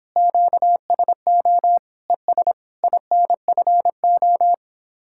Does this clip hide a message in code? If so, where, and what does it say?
Yes: Morse code from 0.2 to 4.5 s, heard as a keyed tone, about 29 wpm, 700 Hz. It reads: QSO ES INFO